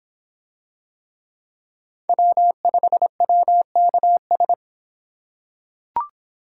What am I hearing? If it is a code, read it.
Morse code, heard as a keyed tone, about 25 wpm, 700 Hz. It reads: W5WKS